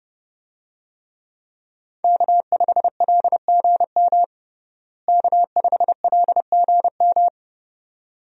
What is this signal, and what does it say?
Morse code, heard as a keyed tone, about 28 wpm, 700 Hz. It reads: K5LGM K5LGM
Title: USA Call Signs - 5 Characters - Repeated 2x - 30wpm